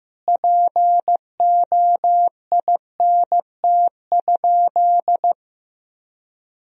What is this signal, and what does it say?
Morse code, heard as a keyed tone, about 16 wpm, 700 Hz. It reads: POINT?
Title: Sentences from Top 1000 Words - Review Part 4 - 15wpm